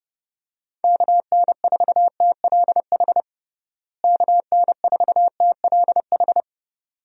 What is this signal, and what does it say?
Morse code, heard as a keyed tone, about 30 wpm, 700 Hz. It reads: KN4TLH KN4TLH
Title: USA Call Signs - 6 Characters - Repeated 3x - 30wpm